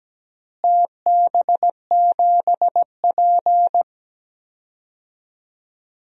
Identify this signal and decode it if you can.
Morse code, heard as a keyed tone, about 17 wpm, 700 Hz. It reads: TB7P
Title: DXCC-100 Call Signs - 4 Characters - Repeated 3x - 17wpm